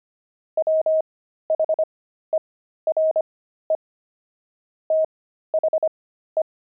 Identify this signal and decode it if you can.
Morse code, heard as a keyed tone, about 25 wpm, spaced as for 13 wpm, 630 Hz. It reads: WHERE THE
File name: pangram-fs-13-cs-25-beehive